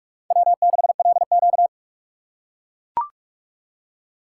Morse code, heard as a keyed tone, about 43 wpm, 700 Hz. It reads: W6LQ
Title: USA Call Signs - 6x Character Spacing - 45wpm